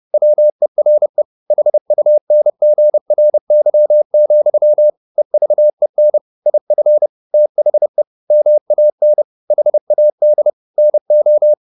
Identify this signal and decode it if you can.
Morse code, heard as a keyed tone, about 30 wpm, 600 Hz. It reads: WERE HUNGRY, EVEN IF THE MAN HAD NO